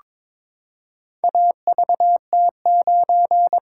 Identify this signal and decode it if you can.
Morse code, heard as a keyed tone, about 22 wpm, 700 Hz. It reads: AVT9
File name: practice / 22wpm / Four Letter-Number Combinations - 22wpm